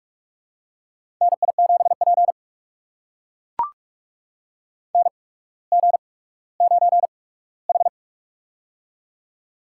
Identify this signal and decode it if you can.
Morse code, heard as a keyed tone, about 44 wpm, spaced as for 19 wpm, 700 Hz. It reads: NI7P NG9H
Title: USA Call Signs - 4 Characters - 8x Character Spacing - 45wpm